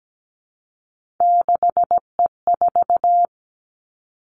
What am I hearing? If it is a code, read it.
Morse code, heard as a keyed tone, about 17 wpm, 700 Hz. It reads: 6E4
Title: Three Challenging Letter-Number Combinations 17wpm